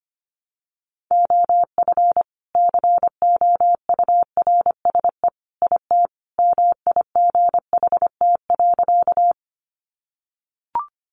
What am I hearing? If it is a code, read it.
Morse code, heard as a keyed tone, about 25 wpm, 700 Hz. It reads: OF COURSE IT MIGHT.
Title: Sentences from Top 500 Words 25wpm